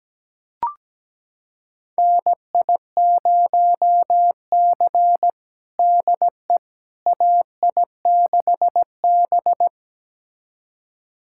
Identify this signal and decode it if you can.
Morse code, heard as a keyed tone, about 17 wpm, 700 Hz. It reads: NI0C DE AI6B
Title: USA Callsign DE Callsign - 4 Characters - 17wpm